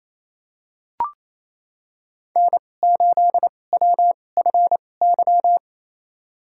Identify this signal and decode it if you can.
Morse code, heard as a keyed tone, about 28 wpm, spaced as for 20 wpm, 700 Hz. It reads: N8WFY
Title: USA Call Signs - 5 Characters - 2x Character Spacing - 28wpm